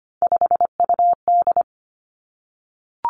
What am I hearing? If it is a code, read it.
Morse code, heard as a keyed tone, about 25 wpm, 700 Hz. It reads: 5UD
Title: Three Challenging Letter-Number Combinations 25wpm